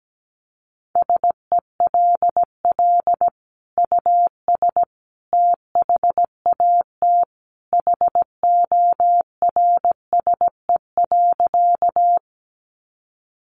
Morse code, heard as a keyed tone, about 17 wpm, 700 Hz. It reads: SELL US THAT HORSE.
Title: Sentences from Top 1100 Words 17wpm